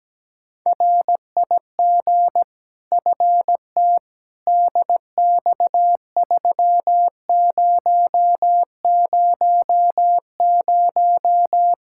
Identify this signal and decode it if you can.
Morse code, heard as a keyed tone, about 17 wpm, 700 Hz. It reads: RIG FT DX3000
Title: Practice QSO - Exchange 2 - Info - 17wpm